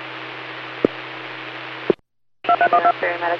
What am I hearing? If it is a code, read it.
DTMF, heard as telephone touch tones, tones of about 80 ms. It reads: 2312